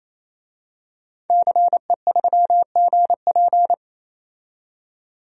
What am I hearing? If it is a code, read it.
Morse code, heard as a keyed tone, about 27 wpm, 700 Hz. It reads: CE3GP
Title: DXCC-100 Call Signs - 5 Characters - 28wpm